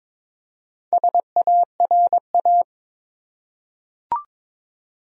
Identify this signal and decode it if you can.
Morse code, heard as a keyed tone, about 22 wpm, 700 Hz. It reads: SARA
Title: USA Common Names - Max 5 Letters - 22wpm